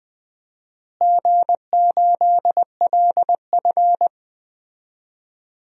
Morse code, heard as a keyed tone, about 20 wpm, 700 Hz. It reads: G8LF
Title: DXCC-100 Call Signs - 4 Characters - Repeated 3x - 20wpm